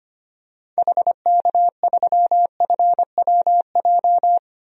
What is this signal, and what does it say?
Morse code, heard as a keyed tone, about 25 wpm, 700 Hz. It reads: HK3FWJ
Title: DXCC-100 Call Signs - 6 Characters - Repeated 2x - 25wpm